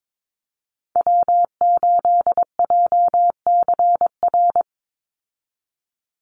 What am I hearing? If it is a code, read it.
Morse code, heard as a keyed tone, about 22 wpm, 700 Hz. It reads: W8JCR
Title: USA Call Signs 22wpm